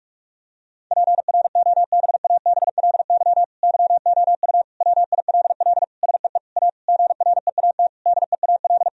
Morse code, heard as a keyed tone, about 43 wpm, 700 Hz. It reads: PROBABLY YOU WILL SEE A GREAT DEAL
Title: Sentences from Top 1000 Words - Review Part 4 - 45wpm